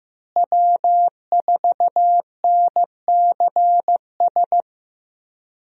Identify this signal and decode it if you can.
Morse code, heard as a keyed tone, about 15 wpm, 700 Hz. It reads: W4NCS
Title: USA Call Signs - 5 Characters - Repeated 3x - 15wpm